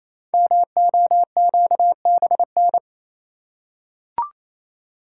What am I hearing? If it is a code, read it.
Morse code, heard as a keyed tone, about 29 wpm, 700 Hz. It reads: MOQBN
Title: Code Groups - 5 Letters - 28wpm